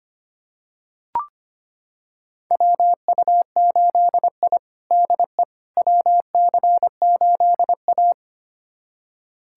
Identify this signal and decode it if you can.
Morse code, heard as a keyed tone, about 25 wpm, 700 Hz. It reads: WU8I DE WC8A
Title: USA Callsign DE Callsign - 4 Characters - 25wpm